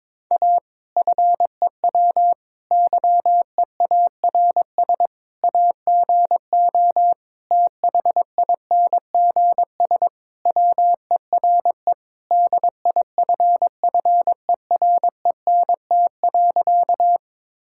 Morse code, heard as a keyed tone, about 21 wpm, 700 Hz. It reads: A FEW YEARS AGO THINGS WERE DIFFERENT.
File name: practice / 22wpm / Sentences with 4-Word Phrases - 22wpm